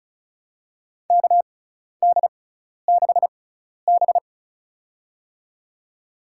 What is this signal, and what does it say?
Morse code, heard as a keyed tone, about 36 wpm, spaced as for 12 wpm, 700 Hz. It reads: KD6B
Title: USA Call Signs - 4 Characters - 6x Character Spacing - 35wpm